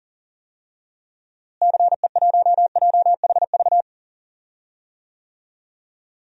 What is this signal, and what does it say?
Morse code, heard as a keyed tone, about 40 wpm, 700 Hz. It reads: CE1JHV
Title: DXCC-100 Call Signs - 6 Characters - Repeated 3x - 40wpm